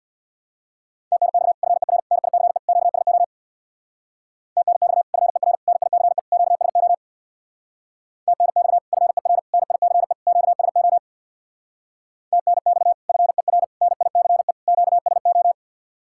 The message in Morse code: TNX FER NICE QSO TNX FER NICE QSO TNX FER NICE QSO TNX FER NICE QSO

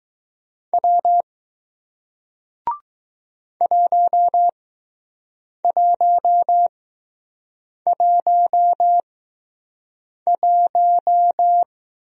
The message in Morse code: W 1111

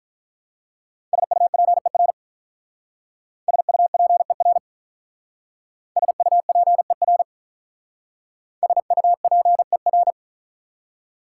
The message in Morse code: SUPER SUPER SUPER SUPER

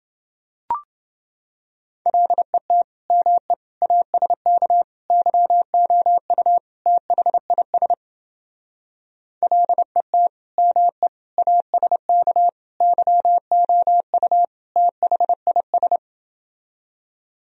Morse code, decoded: LET ME ASK YOU THIS LET ME ASK YOU THIS

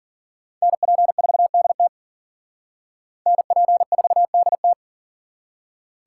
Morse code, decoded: NP4DT NP4DT

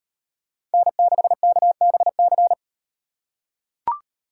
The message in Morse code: N6KBC

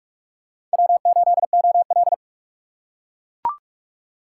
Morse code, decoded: W8OL

